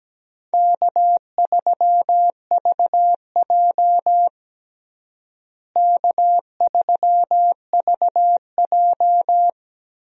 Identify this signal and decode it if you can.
Morse code, heard as a keyed tone, about 17 wpm, 700 Hz. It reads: K3VJ K3VJ
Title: USA Call Signs - 4 Characters - Repeated 3x - 17wpm